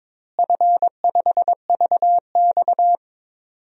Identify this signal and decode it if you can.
Morse code, heard as a keyed tone, about 22 wpm, 700 Hz. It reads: F5VX
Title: DXCC-100 Call Signs - 4 Characters - Repeated 2x - 22wpm